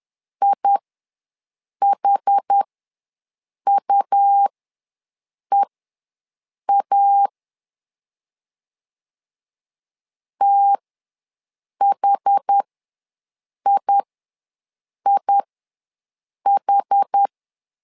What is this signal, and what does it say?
Morse code, heard as a keyed tone, about 10 wpm, spaced as for 6 wpm, 800 Hz. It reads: IHUEA THIIH
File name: LEZIONI / 2 UHDN / 2 UH ws 10wpm 2029